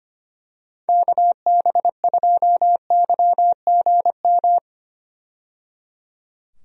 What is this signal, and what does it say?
Morse code, heard as a keyed tone, about 25 wpm, 700 Hz. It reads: KB2YGM
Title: USA Call Signs - 6 Characters - 25wpm